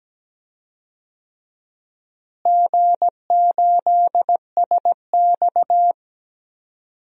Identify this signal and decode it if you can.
Morse code, heard as a keyed tone, about 17 wpm, 700 Hz. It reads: G8SX
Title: DXCC-100 Call Signs - 4 Characters - Repeated 2x - 17wpm